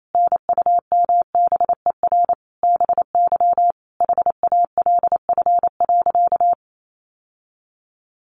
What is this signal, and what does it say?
Morse code, heard as a keyed tone, about 28 wpm, 700 Hz. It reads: NUMBER BY HALF.
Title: Sentences from Top 1100 Words 28wpm